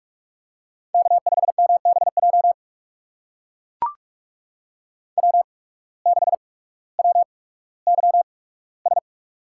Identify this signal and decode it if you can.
Morse code, heard as a keyed tone, about 44 wpm, spaced as for 19 wpm, 700 Hz. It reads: K5MBJ W6WYS
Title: USA Call Signs - 5 Characters - 8x Character Spacing - 45wpm